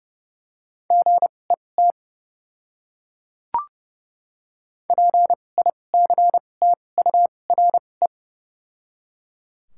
Morse code, heard as a keyed tone, about 31 wpm, 700 Hz. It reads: G E T P I C T U R E
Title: 2x Character Spacing - Top 200 Words - 30wpm